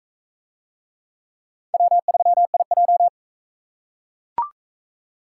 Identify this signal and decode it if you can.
Morse code, heard as a keyed tone, about 37 wpm, 700 Hz. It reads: W3IJ